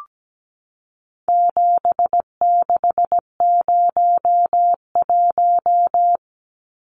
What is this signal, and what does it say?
Morse code, heard as a keyed tone, about 17 wpm, 700 Hz. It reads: 7601
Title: Numbers - 4-Digit 17wpm